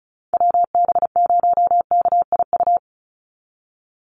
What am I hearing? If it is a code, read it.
Morse code, heard as a keyed tone, about 34 wpm, 700 Hz. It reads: WB0KIU
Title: NA SOTA Chaser Callsigns 35wpm